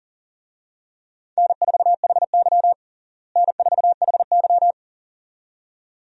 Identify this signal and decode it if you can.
Morse code, heard as a keyed tone, about 40 wpm, 700 Hz. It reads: N4HY N4HY
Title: USA Call Signs - 4 Characters - Repeated 2x - 40wpm